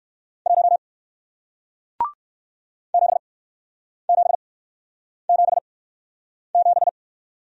Morse code, heard as a keyed tone, about 60 wpm, spaced as for 9 wpm, 700 Hz. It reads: 1 7777